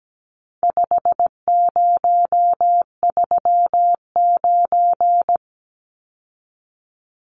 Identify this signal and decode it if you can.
Morse code, heard as a keyed tone, about 17 wpm, 700 Hz. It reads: 5039
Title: Numbers - 4-Digit 17wpm